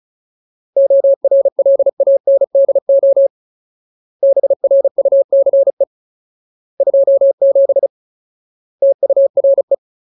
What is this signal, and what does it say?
Morse code, heard as a keyed tone, about 34 wpm, 550 Hz. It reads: ORLANDO BRUCE 27 TURE
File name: Ultimate Word List / 35WPM Ultimate